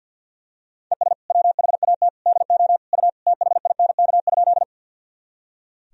Morse code, heard as a keyed tone, about 46 wpm, 700 Hz. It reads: ES WHAT DO U THINK?
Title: Sentences from Top 200 Words and Abbreviations 50wpm